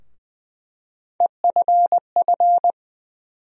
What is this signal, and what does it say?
Morse code, heard as a keyed tone, about 20 wpm, 700 Hz. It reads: EFF
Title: USA Call Sign Postfixes - 3 Letters 20wpm